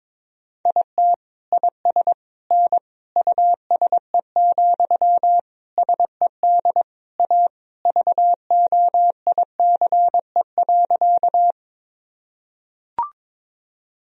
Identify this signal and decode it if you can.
Morse code, heard as a keyed tone, about 22 wpm, 700 Hz. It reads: IT IS N USE, SED A VOICE.
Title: Sentences from Top 500 Words and Abbreviations 22wpm